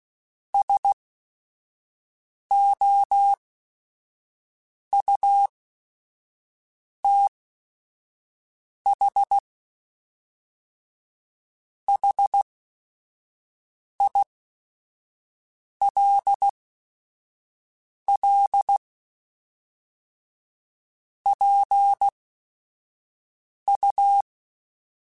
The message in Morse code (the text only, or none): SOUTH HILL PU